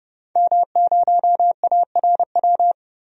Morse code, heard as a keyed tone, about 32 wpm, 700 Hz. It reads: M0ARW
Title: DXCC-100 Call Signs - 5 Characters - 30wpm